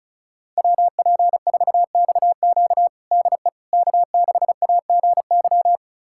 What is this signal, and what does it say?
Morse code, heard as a keyed tone, about 36 wpm, 700 Hz. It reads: WP4XQ DE K6AGY
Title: USA Callsign DE Callsign 35wpm